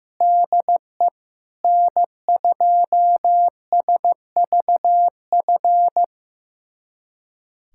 Morse code, heard as a keyed tone, about 15 wpm, 700 Hz. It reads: DE N2SVF
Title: USA Callsign DE Callsign 15wpm